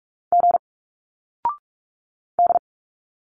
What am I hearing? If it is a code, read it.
Morse code, heard as a keyed tone, about 46 wpm, 700 Hz. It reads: G D
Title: Single Letters 45wpm